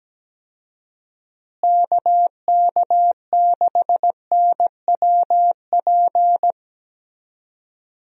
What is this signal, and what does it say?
Morse code, heard as a keyed tone, about 17 wpm, 700 Hz. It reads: KK6NWP
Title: USA Call Signs - 6 Characters - Repeated 2x - 17wpm